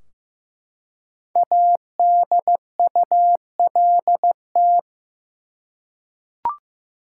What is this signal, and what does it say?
Morse code, heard as a keyed tone, about 15 wpm, 700 Hz. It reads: ADULT
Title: Five Letter Words - 15wpm